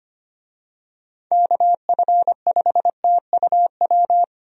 Morse code, heard as a keyed tone, about 25 wpm, 700 Hz. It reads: KF5TUW
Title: USA Call Signs - 6 Characters - Repeated 3x - 25wpm